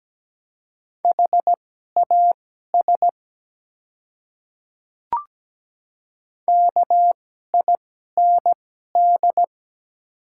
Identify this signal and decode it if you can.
Morse code, heard as a keyed tone, about 17 wpm, spaced as for 12 wpm, 700 Hz. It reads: HAS KIND
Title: 2x Character Spacing - Top 200 Words - 17wpm